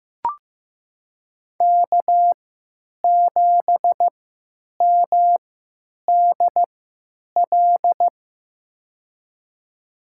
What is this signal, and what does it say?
Morse code, heard as a keyed tone, about 15 wpm, spaced as for 9 wpm, 700 Hz. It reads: K7MDL
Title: USA Call Signs - 5 Characters - 3x Character Spacing - 15wpm